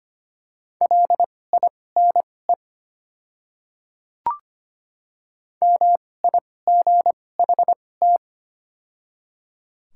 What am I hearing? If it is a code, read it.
Morse code, heard as a keyed tone, about 25 wpm, spaced as for 18 wpm, 700 Hz. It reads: LINE MIGHT
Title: 2x Character Spacing - Top 500 Words - 25wpm